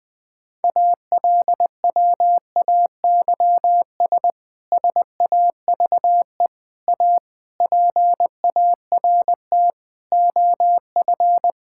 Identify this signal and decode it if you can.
Morse code, heard as a keyed tone, about 20 wpm, 700 Hz. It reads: ALWAYS SAVE A PART OF
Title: Difficult Sentences with Hints - 20wpm